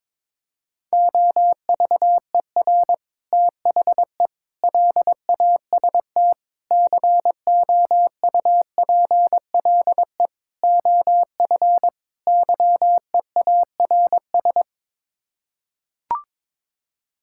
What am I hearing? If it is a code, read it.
Morse code, heard as a keyed tone, about 22 wpm, 700 Hz. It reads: OVER THE LAST COUPLE OF YEARS
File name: practice / 22wpm / Sets of 6 Words - 22wpm